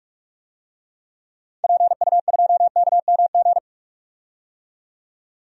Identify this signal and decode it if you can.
Morse code, heard as a keyed tone, about 41 wpm, 700 Hz. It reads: PU2KMG
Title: DXCC-100 Call Signs - 6 Characters - 45wpm